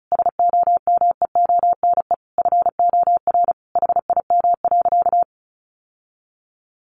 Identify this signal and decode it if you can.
Morse code, heard as a keyed tone, about 34 wpm, 700 Hz. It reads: SOMEONE FOR HIM.
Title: Sentences from Top 600 Words 35wpm